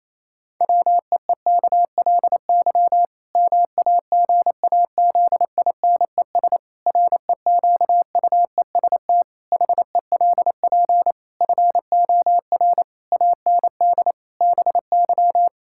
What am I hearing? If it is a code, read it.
Morse code, heard as a keyed tone, about 27 wpm, 700 Hz. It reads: WEEKLY MAGAZINES REQUEST HELP FOR AND BY
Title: Pangram Challenge v2 - 28wpm